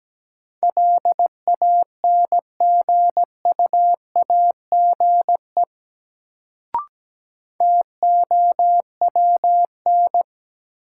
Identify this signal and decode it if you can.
Morse code, heard as a keyed tone, about 17 wpm, 700 Hz. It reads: LANGUAGE TOWN
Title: Sentences from Top 500 Words - New Word Drill - 17wpm